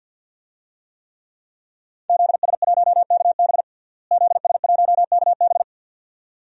Morse code, heard as a keyed tone, about 53 wpm, 700 Hz. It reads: ZS1KB ZS1KB
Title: DXCC-100 Call Signs - 5 Characters - Repeated 2x - 50wpm